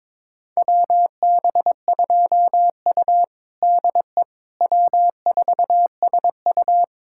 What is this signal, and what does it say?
Morse code, heard as a keyed tone, about 22 wpm, 700 Hz. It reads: WB2U DE W4SU
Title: USA Callsign DE Callsign - 4 Characters - 22wpm